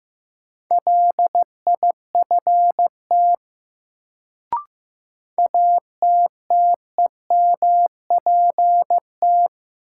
Morse code, heard as a keyed tone, about 16 wpm, 700 Hz. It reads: LIFT ATTEMPT
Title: Sentences from Top 1000 Words - New Word Drill - 15wpm